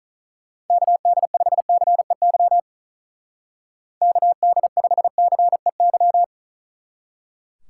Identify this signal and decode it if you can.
Morse code, heard as a keyed tone, about 44 wpm, 700 Hz. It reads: KD5CEY KD5CEY
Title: Speed Racing - USA 2x3 Call Signs 35wpm